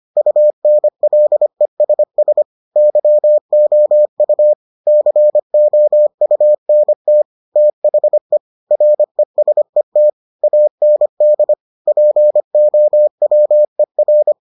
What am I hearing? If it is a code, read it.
Morse code, heard as a keyed tone, about 25 wpm, 600 Hz. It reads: UNLESS YOU COUNT THE RESET AND POWER